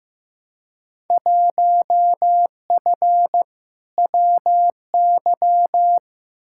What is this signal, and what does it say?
Morse code, heard as a keyed tone, about 15 wpm, 700 Hz. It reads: 1F WY
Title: Field Day Exchanges Encore - 15wpm